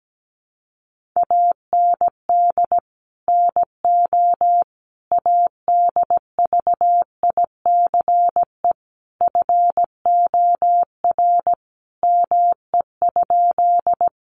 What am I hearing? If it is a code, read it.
Morse code, heard as a keyed tone, about 17 wpm, 700 Hz. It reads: AND NO ADVICE FOR ME?
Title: Sentences from Top 1100 Words 17wpm